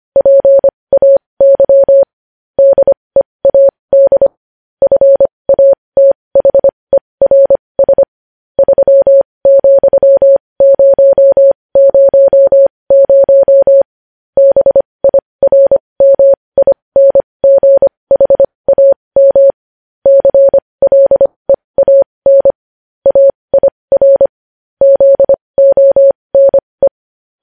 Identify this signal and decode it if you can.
Morse code, heard as a keyed tone, about 25 wpm, spaced as for 20 wpm, 550 Hz. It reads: PAY DEAD FATHERS 3,000 BIRMINGHAM CLEAN AIR ZONE
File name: podcast / News headlines-1x-25x20